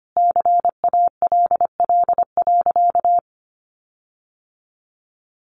Morse code, heard as a keyed tone, about 25 wpm, 700 Hz. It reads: CALL.
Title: Sentences from Top 1300 Words 25wpm